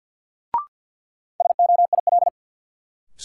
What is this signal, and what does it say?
Morse code, heard as a keyed tone, about 52 wpm, 700 Hz. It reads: SOIL
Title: Sentences from Top 1500 Words - New Word Drill Alt - 50wpm